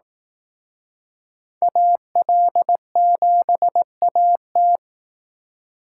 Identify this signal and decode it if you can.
Morse code, heard as a keyed tone, about 19 wpm, 700 Hz. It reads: AL7AT
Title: Speed Racing - USA 2x2 Call Signs 15wpm